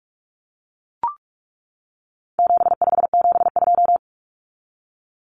Morse code, heard as a keyed tone, about 44 wpm, 700 Hz. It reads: 7572
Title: Numbers - 4-Digit 45wpm